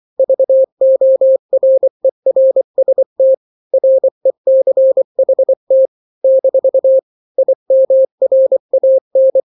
Morse code, heard as a keyed tone, about 24 wpm, 530 Hz. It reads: VORERST RECHT - IMRAN
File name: daily / 23wpm